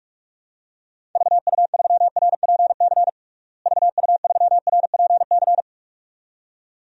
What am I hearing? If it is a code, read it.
Morse code, heard as a keyed tone, about 43 wpm, 700 Hz. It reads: VU3RPC VU3RPC
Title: DXCC-100 Call Signs - 6 Characters - Repeated 2x - 45wpm